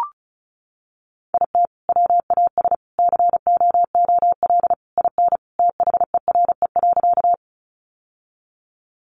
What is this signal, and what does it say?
Morse code, heard as a keyed tone, about 34 wpm, 700 Hz. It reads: IT WAS COOL IN THERE.
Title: Sentences from Top 1200 Words 35wpm